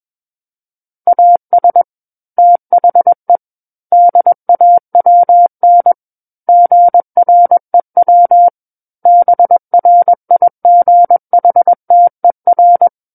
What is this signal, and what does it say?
Morse code, heard as a keyed tone, about 21 wpm, 700 Hz. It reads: AS THE DAWN GREW BRIGHTER
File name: Book One / 12-Chapter 12 - 21 WPM